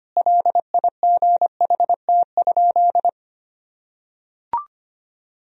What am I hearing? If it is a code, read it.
Morse code, heard as a keyed tone, about 25 wpm, 700 Hz. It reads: LIGHT?
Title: Sentences from Top 1000 Words - Review Part 1 - 25wpm